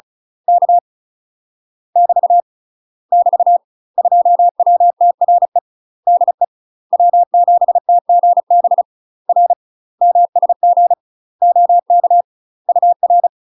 Morse code, heard as a keyed tone, about 35 wpm, 700 Hz. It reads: K <BT> <BT> 2WTRE DE W7TGB R MSG OK UR
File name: UK MASTER QSOs TRAINING / 35WPM / UK MASTER QSOs 0011